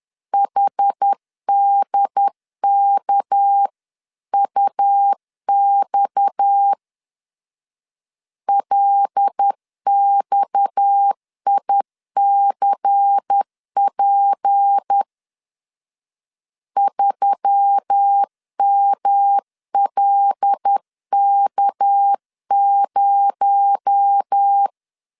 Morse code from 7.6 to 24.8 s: LXICP 3MLK0